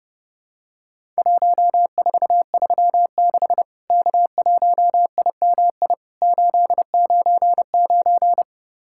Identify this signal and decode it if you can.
Morse code, heard as a keyed tone, about 30 wpm, 700 Hz. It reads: 1436 K1IMI 899
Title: Worked All Europe DX Contest - QTC Exchange - 30wpm